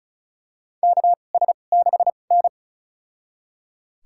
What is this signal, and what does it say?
Morse code, heard as a keyed tone, about 36 wpm, spaced as for 25 wpm, 700 Hz. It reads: KS6N
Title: USA Call Signs - 4 Characters - 2x Character Spacing - 35wpm